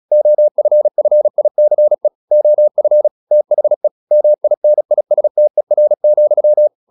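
Morse code, heard as a keyed tone, about 36 wpm, 600 Hz. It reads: OFFICE OF THE MINISTER,
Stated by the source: Morse practice at 36 wpm, text of Ken Follett's novel Never, from CW med Gnister.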